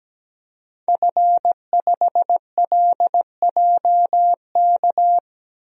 Morse code, heard as a keyed tone, about 17 wpm, 700 Hz. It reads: F5LJK
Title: DXCC-100 Call Signs - 5 Characters - 17wpm